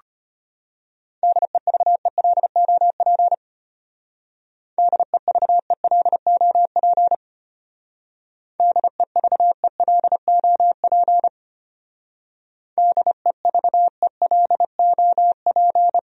Morse code, heard as a keyed tone, about 31 wpm, 700 Hz. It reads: DEVELOP DEVELOP DEVELOP DEVELOP
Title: Speed-Racing Top 500 Words 25wpm